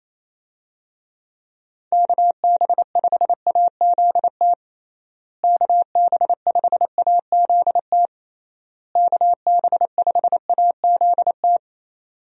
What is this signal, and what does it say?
Morse code, heard as a keyed tone, about 28 wpm, 700 Hz. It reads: KB5AZT KB5AZT KB5AZT
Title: USA Call Signs - 6 Characters - Repeated 3x - 28wpm